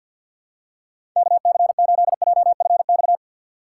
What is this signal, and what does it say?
Morse code, heard as a keyed tone, about 53 wpm, 700 Hz. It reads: KC8JFX
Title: USA Call Signs - 4x Character Spacing - 50wpm